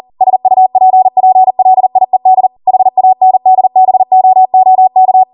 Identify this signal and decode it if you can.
Morse code, heard as a keyed tone, about 40 wpm, 750 Hz. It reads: SUPPLIED HANDBOOK